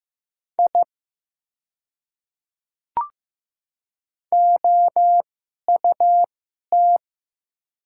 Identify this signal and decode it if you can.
Morse code, heard as a keyed tone, about 15 wpm, spaced as for 11 wpm, 700 Hz. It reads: I OUT
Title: 2x Character Spacing - Top 200 Words - 15wpm